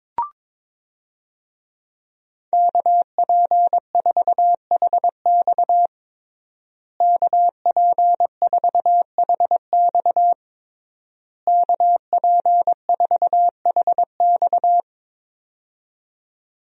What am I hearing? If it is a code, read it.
Morse code, heard as a keyed tone, about 22 wpm, 700 Hz. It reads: KP4HX KP4HX KP4HX
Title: USA Call Signs - 5 Characters - Repeated 3x - 22wpm